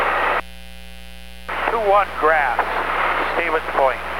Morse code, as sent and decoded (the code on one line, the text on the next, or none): none